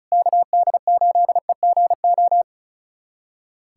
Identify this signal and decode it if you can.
Morse code, heard as a keyed tone, about 36 wpm, 700 Hz. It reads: KD8EGO
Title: USA Callsign DE Callsign 35wpm